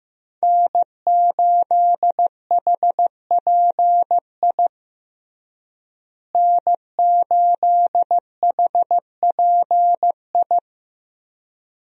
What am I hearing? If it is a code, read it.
Morse code, heard as a keyed tone, about 15 wpm, 700 Hz. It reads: N8HPI N8HPI
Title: USA Call Signs - 5 Characters - Repeated 2x - 15wpm